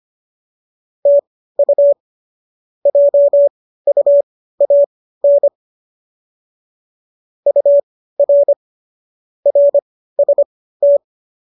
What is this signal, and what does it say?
Morse code, heard as a keyed tone, about 25 wpm, spaced as for 15 wpm, 570 Hz. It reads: TU JUAN UR RST
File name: qso204 15